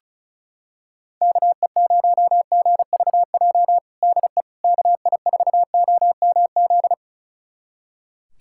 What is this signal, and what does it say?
Morse code, heard as a keyed tone, about 36 wpm, 700 Hz. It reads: KE0GVJ DE KI4OMZ
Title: USA Callsign DE Callsign 35wpm